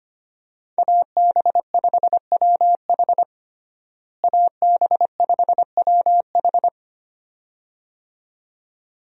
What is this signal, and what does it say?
Morse code, heard as a keyed tone, about 25 wpm, 700 Hz. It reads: AB5WH AB5WH
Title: USA Call Signs - 5 Characters - Repeated 2x - 25wpm